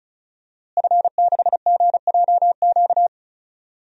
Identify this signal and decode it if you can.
Morse code, heard as a keyed tone, about 36 wpm, 700 Hz. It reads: F6GJQ
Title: DXCC-100 Call Signs - 5 Characters - Repeated 2x - 35wpm